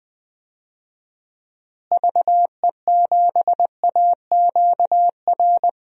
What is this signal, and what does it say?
Morse code, heard as a keyed tone, about 20 wpm, 700 Hz. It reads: VE7AQR
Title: DXCC-100 Call Signs - 6 Characters - Repeated 2x - 20wpm